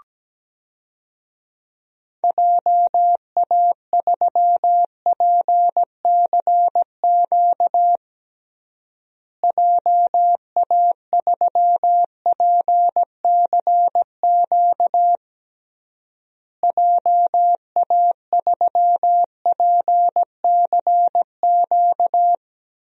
Morse code, heard as a keyed tone, about 17 wpm, 700 Hz. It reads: JA3PCQ JA3PCQ JA3PCQ
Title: DXCC-100 Call Signs - 6 Characters - Repeated 3x - 17wpm